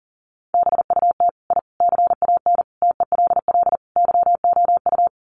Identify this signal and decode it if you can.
Morse code, heard as a keyed tone, about 40 wpm, 700 Hz. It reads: BUT I CAN TELL YOU
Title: Sets of 5 Words Encore 40wpm